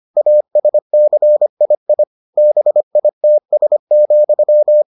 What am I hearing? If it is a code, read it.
Morse code, heard as a keyed tone, about 25 wpm, 600 Hz. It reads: ASCII BITS,